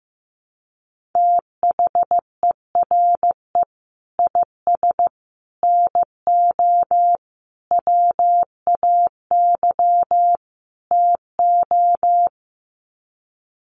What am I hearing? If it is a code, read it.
Morse code, heard as a keyed tone, about 15 wpm, 700 Hz. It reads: THERE IS NO WAY TO
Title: Sets of 5 Words Encore 15wpm